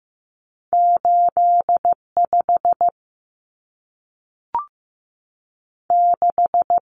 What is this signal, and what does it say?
Morse code, heard as a keyed tone, about 15 wpm, 700 Hz. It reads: 85 6
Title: Common Cut Numbers - 2-Digit 15wpm